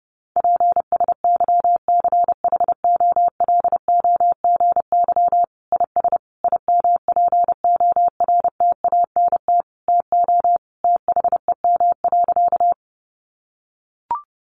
Morse code, heard as a keyed tone, about 30 wpm, 700 Hz. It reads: PSYCHOLOGY IS IMPORTANT TO THEM.
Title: Sentences from Top 1400 Words 30wpm